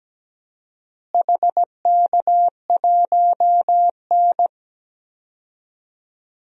Morse code, heard as a keyed tone, about 17 wpm, 700 Hz. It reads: HK1N
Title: DXCC-100 Call Signs - 4 Characters - Repeated 3x - 17wpm